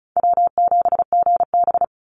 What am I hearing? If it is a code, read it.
Morse code, heard as a keyed tone, about 34 wpm, 700 Hz. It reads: W7GB
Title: Call Signs - 35wpm